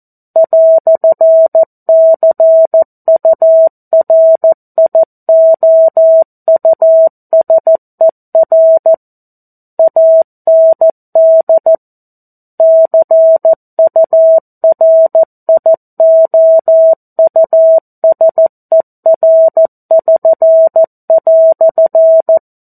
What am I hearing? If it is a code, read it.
Morse code, heard as a keyed tone, about 14 wpm, 650 Hz. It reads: "CURIOUSER AND CURIOUSER<SN>"
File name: 02-Chapter-02 14 WPM